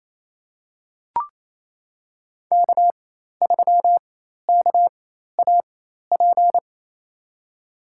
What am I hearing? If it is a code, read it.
Morse code, heard as a keyed tone, about 27 wpm, spaced as for 13 wpm, 700 Hz. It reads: K3KAP
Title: USA Call Signs - 5 Characters - 4x Character Spacing - 28wpm